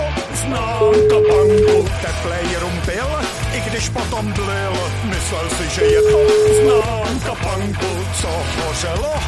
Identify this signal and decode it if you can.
Ringback tone (Europe, 425 Hz), with speech.